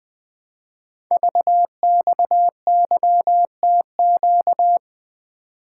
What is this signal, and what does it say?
Morse code, heard as a keyed tone, about 20 wpm, 700 Hz. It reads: VXYTQ
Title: Code Groups - 5 Letter:Numbers - 20wpm